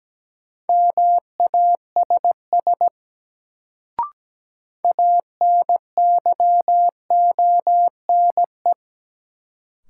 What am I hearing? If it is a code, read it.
Morse code, heard as a keyed tone, about 17 wpm, 700 Hz. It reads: MASS ANYONE
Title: Sets of 5 Words Encore - New Word Drill - 17wpm